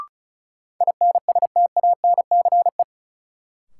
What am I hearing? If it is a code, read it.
Morse code, heard as a keyed tone, about 36 wpm, 700 Hz. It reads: INSTANCE
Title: Sentences from Top 800 Words - New Word Drill - 35wpm